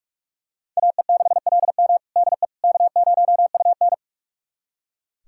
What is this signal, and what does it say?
Morse code, heard as a keyed tone, about 46 wpm, 700 Hz. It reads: AE6LM DE K0UN
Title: USA Callsign DE Callsign 45wpm